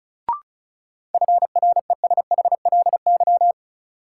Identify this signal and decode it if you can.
Morse code, heard as a keyed tone, about 34 wpm, 700 Hz. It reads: FRESHLY
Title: Sets of 5 Words Encore - New Word Drill - 35wpm